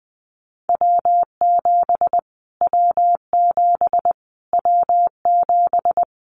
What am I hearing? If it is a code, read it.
Morse code, heard as a keyed tone, about 20 wpm, 700 Hz. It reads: W7 W7 W7
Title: Callsign Prefixes for Top 25 DXCC - Repeated 3x - 20wpm